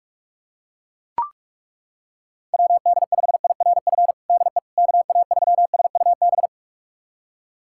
Morse code, heard as a keyed tone, about 41 wpm, 700 Hz. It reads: WD5IRF DE KA3SUB
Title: USA Callsign DE Callsign - 6 Characters - 45wpm